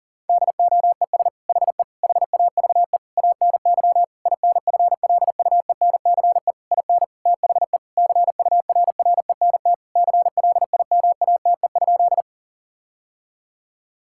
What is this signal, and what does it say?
Morse code, heard as a keyed tone, about 40 wpm, 700 Hz. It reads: DOES HE HAVE ANY INFLUENCE IN THE CURRENT CLIMATE?
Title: Sentences from Top 1600 Words 40wpm